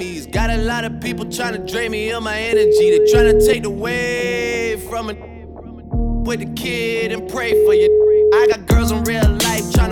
Ringback tone (Europe, 425 Hz), with noise.